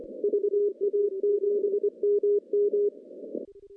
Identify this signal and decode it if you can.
Morse code, heard as a keyed tone, about 24 wpm, 410 Hz. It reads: VA7MM
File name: contests / CW-SS-k5tr- -0200